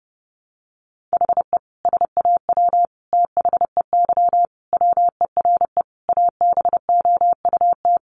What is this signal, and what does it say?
Morse code, heard as a keyed tone, about 28 wpm, 700 Hz. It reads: HE SAW THEY WERE ABOUT
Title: Sentences from Top 700 Words 30wpm